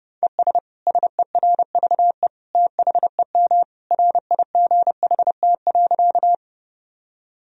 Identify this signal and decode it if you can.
Morse code, heard as a keyed tone, about 32 wpm, 700 Hz. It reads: ES SERVE THEM RIGHT.
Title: Sentences from Top 400 Words and Abbreviations 30wpm